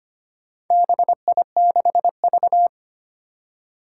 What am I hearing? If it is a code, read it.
Morse code, heard as a keyed tone, about 25 wpm, 700 Hz. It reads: BI6V
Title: Four Challenging Letter-Number Combinations 25wpm